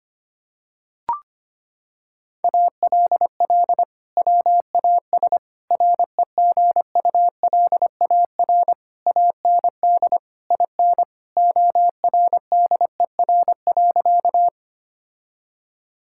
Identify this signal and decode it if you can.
Morse code, heard as a keyed tone, about 25 wpm, 700 Hz. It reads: ALL WAS REGULAR AND IN ORDER.
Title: Sentences from Top 1000 Words - Review Part 6 - 25wpm